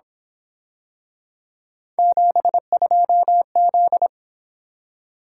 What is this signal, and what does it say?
Morse code, heard as a keyed tone, about 27 wpm, 700 Hz. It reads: 72Z
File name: Speed-Racing - Three Letter-Number Combinations - 22wpm